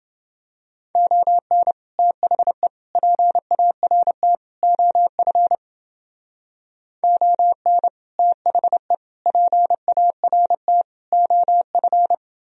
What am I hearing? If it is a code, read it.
Morse code, heard as a keyed tone, about 28 wpm, 700 Hz. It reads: ON THE PART OF ON THE PART OF